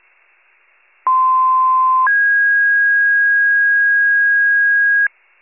Two-tone page, heard as a one-second tone, then a three-second tone, about 1040 then 1650 Hz. It is heard from 1.1 to 5.1 s.